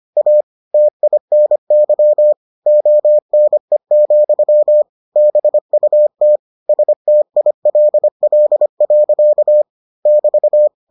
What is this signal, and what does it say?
Morse code, heard as a keyed tone, about 25 wpm, 600 Hz. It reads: A TINY ONE, BUT STILL. <BT>